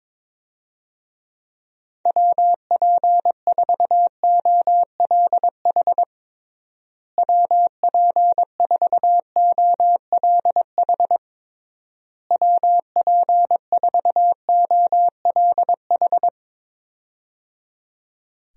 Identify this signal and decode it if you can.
Morse code, heard as a keyed tone, about 22 wpm, 700 Hz. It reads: WP4OLH WP4OLH WP4OLH
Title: USA Call Signs - 6 Characters - Repeated 3x - 22wpm